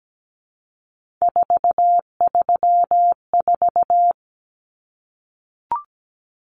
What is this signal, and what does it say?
Morse code, heard as a keyed tone, about 17 wpm, 700 Hz. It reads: 434
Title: Common Cut Numbers - 3-Digit 17wpm